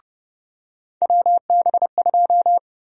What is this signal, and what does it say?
Morse code, heard as a keyed tone, about 30 wpm, 700 Hz. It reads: WB2
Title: USA Call Sign Prefixes - 30wpm